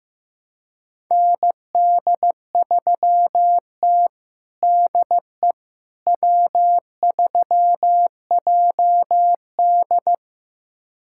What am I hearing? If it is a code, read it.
Morse code, heard as a keyed tone, about 16 wpm, 700 Hz. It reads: ND3T DE W3JD
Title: USA Callsign DE Callsign - 4 Characters - 15wpm